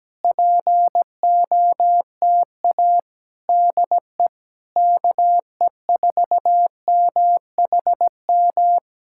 Morse code, heard as a keyed tone, about 17 wpm, 700 Hz. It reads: POTA DE KE4MHM
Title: Speed-Racing Common QSO Phrases - 17wpm